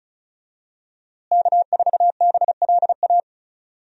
Morse code, heard as a keyed tone, about 36 wpm, 700 Hz. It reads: K4BLA
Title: USA Call Signs - 5 Characters - Repeated 3x - 35wpm